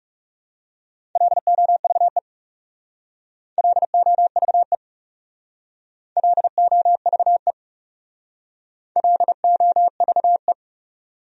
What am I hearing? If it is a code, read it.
Morse code, heard as a keyed tone, about 40 wpm, 700 Hz. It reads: LOVE LOVE LOVE LOVE